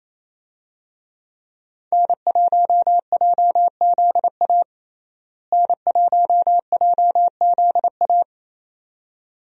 Morse code, heard as a keyed tone, about 28 wpm, 700 Hz. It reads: N1JZA N1JZA
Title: USA Call Signs - 5 Characters - Repeated 2x - 28wpm